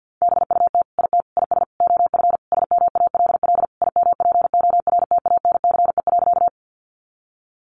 Morse code, heard as a keyed tone, about 50 wpm, 700 Hz. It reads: BUT IT IS OF SMALL IMPORTANCE.
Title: Sentences from Top 1200 Words 50wpm